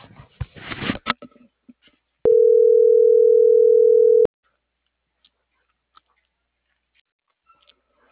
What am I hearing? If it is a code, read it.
Ringback tone (North America), 440 + 480 Hz.